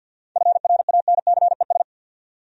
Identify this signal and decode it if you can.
Morse code, heard as a keyed tone, about 53 wpm, 700 Hz. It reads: FRANCES
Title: USA Common Names - 50wpm